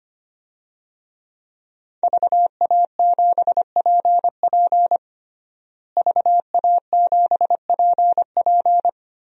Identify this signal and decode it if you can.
Morse code, heard as a keyed tone, about 25 wpm, 700 Hz. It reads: VA7PP VA7PP
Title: DXCC-100 Call Signs - 5 Characters - Repeated 3x - 25wpm